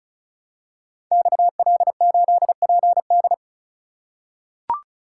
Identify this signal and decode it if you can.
Morse code, heard as a keyed tone, about 36 wpm, 700 Hz. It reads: XL8PD
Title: Code Groups - 5 Letter:Numbers - 35wpm